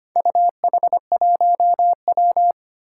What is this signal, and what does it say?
Morse code, heard as a keyed tone, about 25 wpm, 700 Hz. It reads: UH1W
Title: Four Challenging Letter-Number Combinations 25wpm